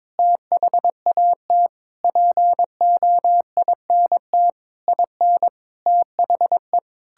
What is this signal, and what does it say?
Morse code, heard as a keyed tone, about 21 wpm, 700 Hz. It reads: THAT POINT IN THE
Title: Sentences from Top 1000 Words - Review Part 5 - 22wpm